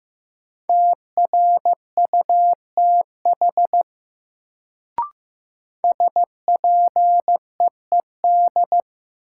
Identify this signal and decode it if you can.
Morse code, heard as a keyed tone, about 15 wpm, 700 Hz. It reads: TRUTH SPEED
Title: Sentences from Top 800 Words - New Word Drill - 15wpm